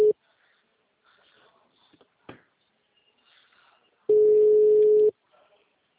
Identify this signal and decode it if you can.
Ringback tone (Europe, 425 Hz).
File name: out- -2016- - - .22345